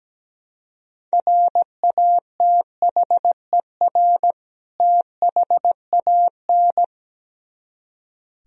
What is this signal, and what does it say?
Morse code, heard as a keyed tone, about 17 wpm, 700 Hz. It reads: RATHER THAN
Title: Speed-Racing - Sets of 2 Words 17wpm